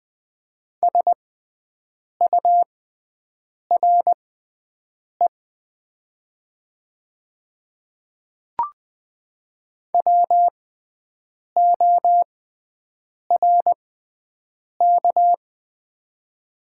Morse code, heard as a keyed tone, about 20 wpm, spaced as for 7 wpm, 700 Hz. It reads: SURE WORK
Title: 6x Character Spacing - Top 300 Words - 20wpm